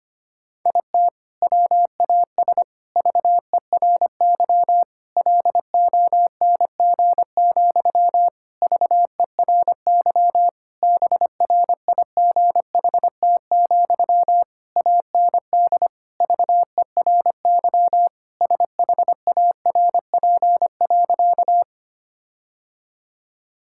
Morse code, heard as a keyed tone, about 25 wpm, 700 Hz. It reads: IT WAS VERY LONG, VERY BRIGHT, AND VERY SHARP.
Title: Sentences from Top 1600 Words 25wpm